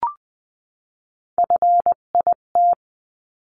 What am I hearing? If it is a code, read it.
Morse code, heard as a keyed tone, about 20 wpm, 700 Hz. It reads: FIT
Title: 258 Lesson 42c - Words 16-20wpm